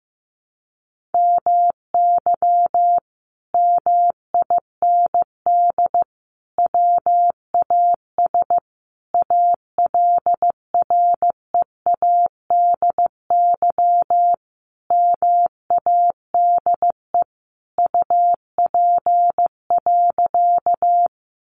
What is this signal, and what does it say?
Morse code, heard as a keyed tone, about 15 wpm, 700 Hz. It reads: MY MIND WAS ALREADY MADE UP.
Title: Sentences from Top 700 Words 15wpm